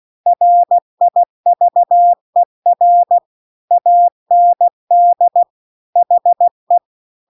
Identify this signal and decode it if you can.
Morse code, heard as a keyed tone, about 16 wpm, 700 Hz. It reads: RIVER AND HE